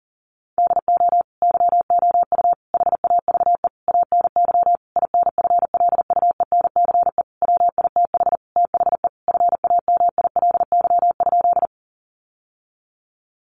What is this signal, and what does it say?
Morse code, heard as a keyed tone, about 40 wpm, 700 Hz. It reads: DO YOU HAVE ANY INFLUENCE WITH THE FAMILY?
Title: Sentences from Top 1100 Words 40wpm